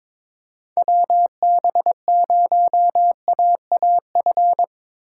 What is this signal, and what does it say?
Morse code, heard as a keyed tone, about 25 wpm, 700 Hz. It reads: WB0AAF
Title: Speed Racing - USA 2x3 Call Signs 22wpm